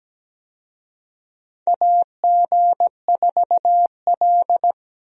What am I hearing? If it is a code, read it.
Morse code, heard as a keyed tone, about 17 wpm, 700 Hz. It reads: AG4L
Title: USA Call Signs - 4 Characters - 17wpm